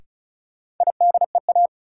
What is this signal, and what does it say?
Morse code, heard as a keyed tone, about 36 wpm, 700 Hz. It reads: IDEA
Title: Sets of 3 Words - New Word Drill - 35wpm